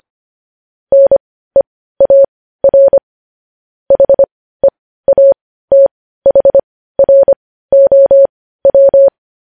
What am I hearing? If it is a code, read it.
Morse code, heard as a keyed tone, about 25 wpm, spaced as for 15 wpm, 550 Hz. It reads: NEAR HEATHROW
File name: podcast / News headlines-1x-25x15